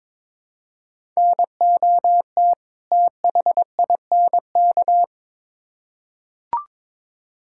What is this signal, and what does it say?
Morse code, heard as a keyed tone, about 21 wpm, 700 Hz. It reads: NOT THINK
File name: Build A Phrase - 2-Word Phrases - 22wpm